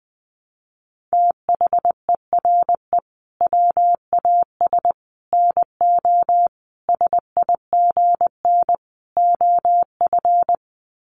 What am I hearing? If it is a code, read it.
Morse code, heard as a keyed tone, about 20 wpm, 700 Hz. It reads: THERE WAS NO SIGN OF
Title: Sets of 5 Words Encore2 20wpm